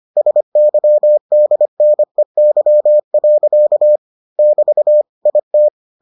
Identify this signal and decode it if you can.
Morse code, heard as a keyed tone, about 25 wpm, 600 Hz. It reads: SYDNEY. <BT> IT